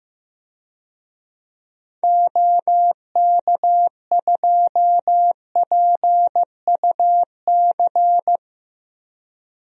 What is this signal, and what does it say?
Morse code, heard as a keyed tone, about 15 wpm, 700 Hz. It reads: OK2PUC
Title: DXCC-100 Call Signs - 6 Characters - Repeated 3x - 15wpm